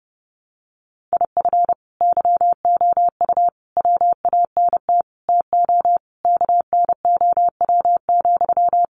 Morse code, heard as a keyed tone, about 30 wpm, 700 Hz. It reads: IF YOU WANT TO KNOW,
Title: Sentences from Top 200 Words 30wpm